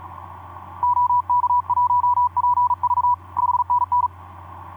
Morse code, heard as a keyed tone, about 37 wpm, 980 Hz. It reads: OK1PV 5NN
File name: d44ac ek2rgt etc 10m cw 23nov2010 1202